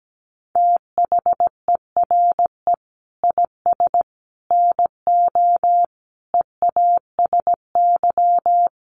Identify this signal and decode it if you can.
Morse code, heard as a keyed tone, about 17 wpm, 700 Hz. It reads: THERE IS NO EASY
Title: Sentences from Top 1300 Words 17wpm